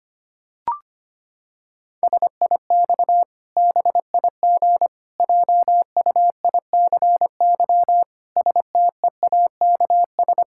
Morse code, heard as a keyed tone, about 25 wpm, 700 Hz. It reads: SIX BIG JUICY STEAKS
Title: Pangram Challenge v2 - 25wpm